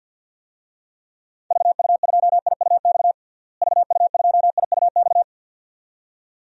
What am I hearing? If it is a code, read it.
Morse code, heard as a keyed tone, about 50 wpm, 700 Hz. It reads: VU2IUX VU2IUX
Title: DXCC-100 Call Signs - 6 Characters - Repeated 2x - 50wpm